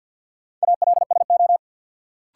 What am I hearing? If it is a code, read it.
Morse code, heard as a keyed tone, about 52 wpm, 700 Hz. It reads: ALSO
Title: Sentences from Top 400 Words - New Word Drill - 50wpm